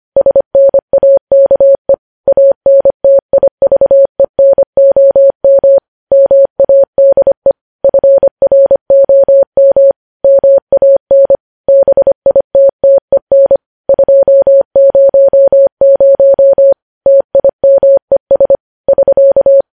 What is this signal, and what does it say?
Morse code, heard as a keyed tone, about 25 wpm, 550 Hz. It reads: SNAKE ANTIVENOM MADE FROM MAN BITTEN 200 TIMES <SK>